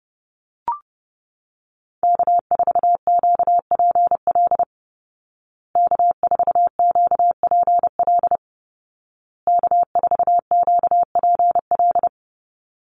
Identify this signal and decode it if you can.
Morse code, heard as a keyed tone, about 30 wpm, 700 Hz. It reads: K4QPL K4QPL K4QPL
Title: Call Signs Repeated 2x - 30wpm